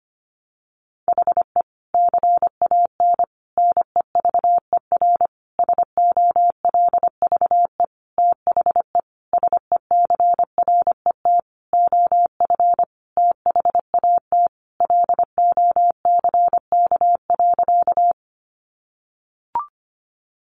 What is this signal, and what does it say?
Morse code, heard as a keyed tone, about 25 wpm, 700 Hz. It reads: HE CAN NEVER SOLVE THE SECRET OF THAT LOCK.
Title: Sentences from Top 1300 Words 25wpm